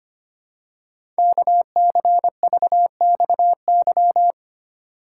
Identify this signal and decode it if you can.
Morse code, heard as a keyed tone, about 25 wpm, 700 Hz. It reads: KCVXY
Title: Code Groups - 5 Letters - 25wpm